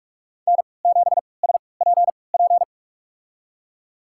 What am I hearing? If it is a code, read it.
Morse code, heard as a keyed tone, about 43 wpm, spaced as for 25 wpm, 700 Hz. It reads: N7SPP
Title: USA Call Signs - 5 Characters - 3x Character Spacing - 45wpm